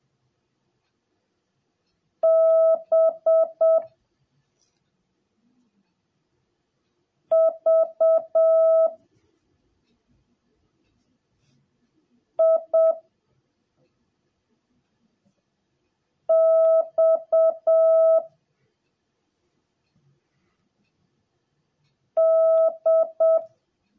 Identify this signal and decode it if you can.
Morse code, heard as a keyed tone, about 7 wpm, 650 Hz. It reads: B V I X D